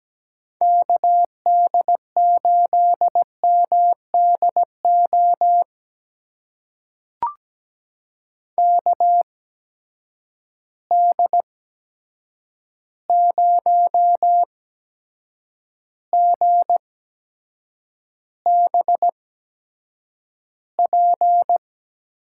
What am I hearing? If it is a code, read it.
Morse code, heard as a keyed tone, about 17 wpm, spaced as for 7 wpm, 700 Hz. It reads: KD8MDO KD0GBP
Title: USA Call Signs - 6 Characters - 8x Character Spacing - 17wpm